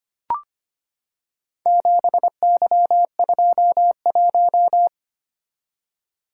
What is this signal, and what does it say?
Morse code, heard as a keyed tone, about 25 wpm, 700 Hz. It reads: 7Y21
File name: practice / 25wpm / Four Letter-Number Combinations - 25wpm